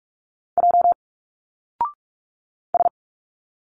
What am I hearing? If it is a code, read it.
Morse code, heard as a keyed tone, about 45 wpm, 700 Hz. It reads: J S
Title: Single Letters 45wpm